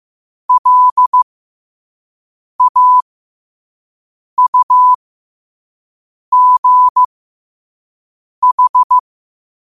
Morse code, heard as a keyed tone, about 15 wpm, spaced as for 5 wpm, 1000 Hz. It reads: LAUGH
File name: qotdfiles / 5wpm